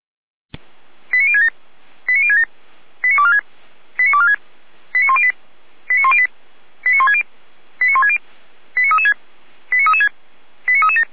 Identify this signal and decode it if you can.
Five-tone selective call (ZVEI), heard as quick runs of five tones, tones of about 80 ms. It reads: 89067 89067 89246 89246 89108 89108 89160 89160 89336 89336 89337